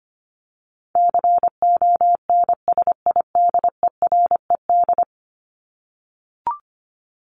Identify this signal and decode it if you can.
Morse code, heard as a keyed tone, about 25 wpm, 700 Hz. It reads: CONSIDERED
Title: ED Suffix - Top 100 Words with Base Words 25wpm